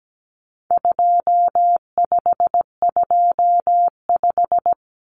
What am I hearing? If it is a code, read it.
Morse code, heard as a keyed tone, about 17 wpm, 700 Hz. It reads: 2525
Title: Common Cut Numbers - 4-Digit 17wpm